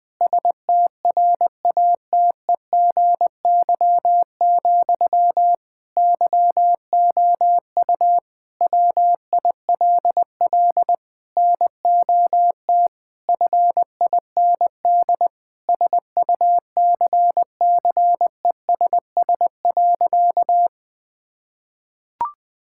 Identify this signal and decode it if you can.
Morse code, heard as a keyed tone, about 20 wpm, 700 Hz. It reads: STRATEGY, YOU WILL NOT FIND SUCCESS.
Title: Sentences from Top 1000 Words - Review Part 5 - 20wpm